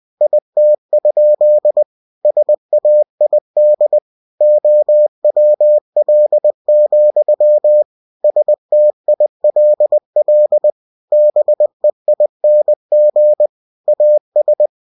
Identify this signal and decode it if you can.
Morse code, heard as a keyed tone, about 20 wpm, 600 Hz. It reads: IT? SAID OWL, STILL BEING AS
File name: cw-books / winnie-the-pooh / winnie-20wpm-page-0102